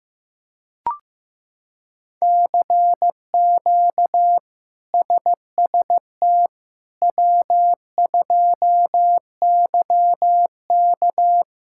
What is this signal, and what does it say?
Morse code, heard as a keyed tone, about 16 wpm, 700 Hz. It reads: CQ SST W2YK
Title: K1USN SST QSO Practice - 15wpm